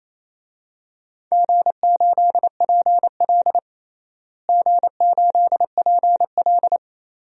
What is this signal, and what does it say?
Morse code, heard as a keyed tone, about 28 wpm, 700 Hz. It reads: G8PL G8PL
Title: DXCC-100 Call Signs - 4 Characters - Repeated 3x - 28wpm